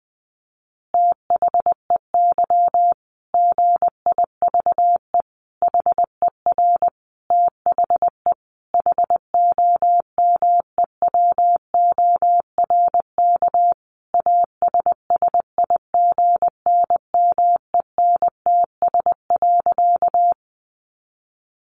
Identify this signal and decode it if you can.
Morse code, heard as a keyed tone, about 20 wpm, 700 Hz. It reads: THEY GIVE HER THE HOMEWORK ASSIGNMENTS.
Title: Easy Phrases 20wpm